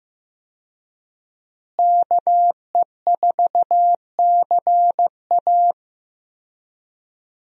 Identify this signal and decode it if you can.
Morse code, heard as a keyed tone, about 15 wpm, 700 Hz. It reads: KE4CA
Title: USA Call Signs - 5 Characters - Repeated 2x - 15wpm